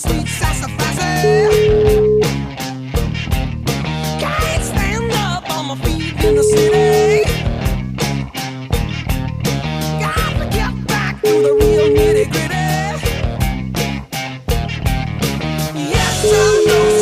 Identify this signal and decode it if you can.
Ringback tone (Europe, 425 Hz), with noise.